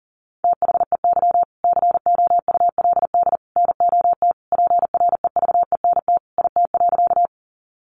Morse code, heard as a keyed tone, about 40 wpm, 700 Hz. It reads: THEY COULD NOT PREVENT IT.
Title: Sentences from Top 1100 Words 40wpm